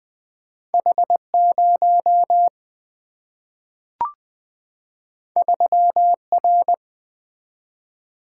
Morse code, heard as a keyed tone, about 20 wpm, 700 Hz. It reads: H 0 3 R
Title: Single-Letter Single-Number 20wpm